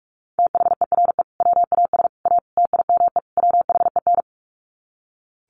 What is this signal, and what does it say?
Morse code, heard as a keyed tone, about 44 wpm, 700 Hz. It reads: THERE WAS A TIME WHEN
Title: Sets of 5 Words 45wpm